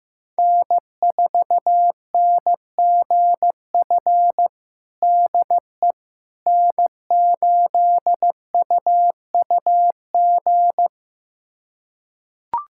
Morse code, heard as a keyed tone, about 15 wpm, 700 Hz. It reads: N4NGF DE N8UUG
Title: USA Callsign DE Callsign - 5 Characters - 15wpm